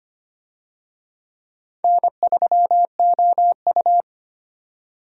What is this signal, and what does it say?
Morse code, heard as a keyed tone, about 25 wpm, 700 Hz. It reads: N3OU
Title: USA Call Signs - 4 Characters - 25wpm